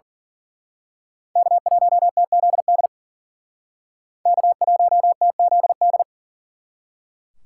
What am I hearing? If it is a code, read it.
Morse code, heard as a keyed tone, about 42 wpm, 700 Hz. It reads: K1TZD K1TZD
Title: Speed Racing - USA Call Signs 40wpm